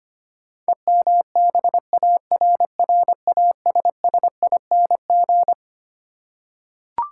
Morse code, heard as a keyed tone, about 25 wpm, 700 Hz. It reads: EMBARRASSING
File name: Twelve Letter Words - 25wpm